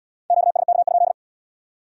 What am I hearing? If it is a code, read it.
Morse code, heard as a keyed tone, about 69 wpm, 700 Hz. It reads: QSK?